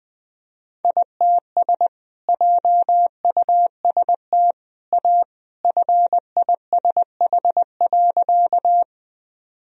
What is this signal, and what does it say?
Morse code, heard as a keyed tone, about 20 wpm, 700 Hz. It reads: ITS JUST A FISH.
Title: Sentences from Top 1000 Words - Review Part 1 - 20wpm